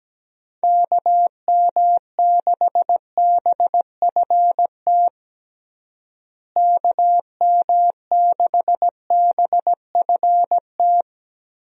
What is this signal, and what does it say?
Morse code, heard as a keyed tone, about 17 wpm, 700 Hz. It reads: KM6BFT KM6BFT
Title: USA Call Signs - 6 Characters - Repeated 3x - 17wpm